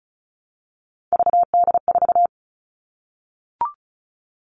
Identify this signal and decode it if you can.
Morse code, heard as a keyed tone, about 36 wpm, 700 Hz. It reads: VD4
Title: Three Challenging Letter-Number Combinations 35wpm